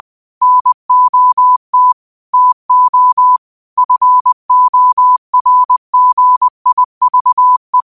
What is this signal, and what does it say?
Morse code, heard as a keyed tone, about 20 wpm, 1000 Hz. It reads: NOT TO FORGIVE
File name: qotdfiles / 20wpm